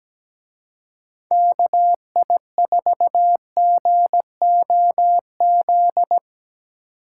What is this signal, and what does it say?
Morse code, heard as a keyed tone, about 19 wpm, 700 Hz. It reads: KI4GOZ
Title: Speed Racing - USA 2x3 Call Signs 17wpm